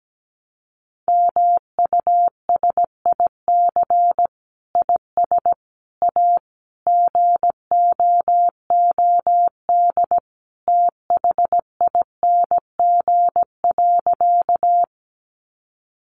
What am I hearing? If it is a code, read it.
Morse code, heard as a keyed tone, about 17 wpm, 700 Hz. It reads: MUSIC IS A GOOD THING.
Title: Sentences from Top 300 Words 17wpm